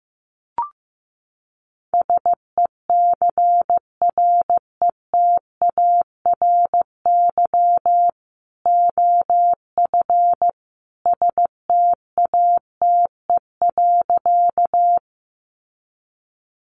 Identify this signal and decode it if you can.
Morse code, heard as a keyed tone, about 15 wpm, 700 Hz. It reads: SECRETARY OF STATE.
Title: Sentences from Top 1300 Words 15wpm